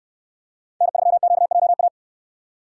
Morse code, heard as a keyed tone, about 69 wpm, 700 Hz. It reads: N3QJA